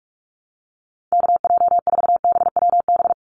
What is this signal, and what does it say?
Morse code, heard as a keyed tone, about 44 wpm, 700 Hz. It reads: KJ4BWB
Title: USA Call Signs 45wpm